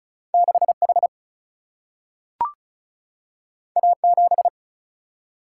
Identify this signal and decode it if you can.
Morse code, heard as a keyed tone, about 36 wpm, 700 Hz. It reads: 6H A7
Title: Single-Letter Single-Number 35wpm